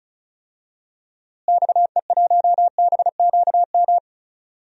Morse code, heard as a keyed tone, about 34 wpm, 700 Hz. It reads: XE1BQM
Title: DXCC-100 Call Signs - 6 Characters - Repeated 2x - 35wpm